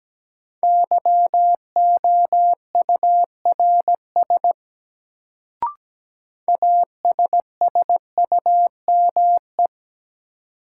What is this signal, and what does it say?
Morse code, heard as a keyed tone, about 17 wpm, 700 Hz. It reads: YOURS ASSUME
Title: Sentences from Top 1500 Words - New Word Drill Alt - 17wpm